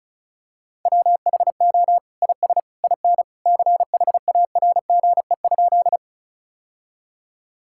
Morse code, heard as a keyed tone, about 36 wpm, 700 Hz. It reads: WHO IS IN CHARGE?
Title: Sentences from Top 1000 Words - Review Part 6 - 35wpm